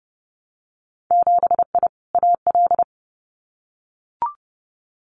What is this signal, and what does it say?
Morse code, heard as a keyed tone, about 30 wpm, 700 Hz. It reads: 7I AL
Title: Winter Field Day Exchanges Encore 30wpm